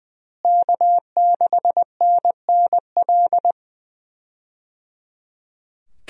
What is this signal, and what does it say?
Morse code, heard as a keyed tone, about 20 wpm, 700 Hz. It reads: K6NNL
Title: USA Call Signs - 5 Characters - Repeated 3x - 20wpm